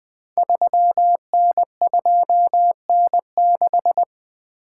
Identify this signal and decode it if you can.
Morse code, heard as a keyed tone, about 20 wpm, 700 Hz. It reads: 3N2N6
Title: Code Groups - 5 Letter:Numbers - 20wpm